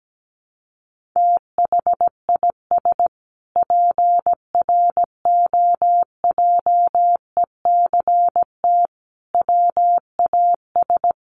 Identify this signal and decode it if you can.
Morse code, heard as a keyed tone, about 17 wpm, 700 Hz. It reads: THIS PROJECT WAS
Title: Sentences from Top 1400 Words 17wpm